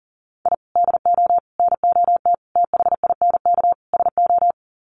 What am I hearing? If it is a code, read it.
Morse code, heard as a keyed tone, about 40 wpm, 700 Hz. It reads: I DO NOT THINK SO
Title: Sets of 5 Words 40wpm